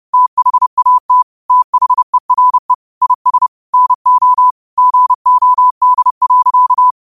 Morse code, heard as a keyed tone, about 30 wpm, 1000 Hz. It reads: THAT THERE IS NO GOD.